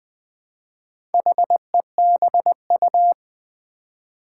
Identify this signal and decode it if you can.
Morse code, heard as a keyed tone, about 20 wpm, 700 Hz. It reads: HEBU
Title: Four Challenging Letter-Number Combinations 20wpm